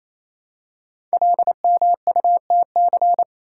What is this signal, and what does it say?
Morse code, heard as a keyed tone, about 27 wpm, 700 Hz. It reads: LMUTC
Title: Code Groups - 5 Letters - 28wpm